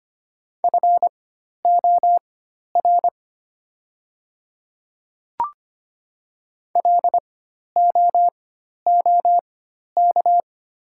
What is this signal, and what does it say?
Morse code, heard as a keyed tone, about 25 wpm, spaced as for 12 wpm, 700 Hz. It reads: FOR LOOK
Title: 4x Character Spacing - Top 100 Words - 25wpm